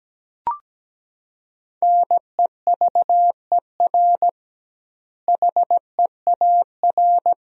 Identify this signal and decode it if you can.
Morse code, heard as a keyed tone, about 17 wpm, 700 Hz. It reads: NEVER HEAR
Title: 2x Word-Spacing - Sentences from Top 400 Words - 17wpm